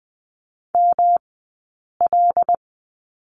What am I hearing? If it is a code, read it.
Morse code, heard as a keyed tone, about 20 wpm, 700 Hz. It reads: M L
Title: 106 Lesson 18d - Letter m - Two Characters 20wpm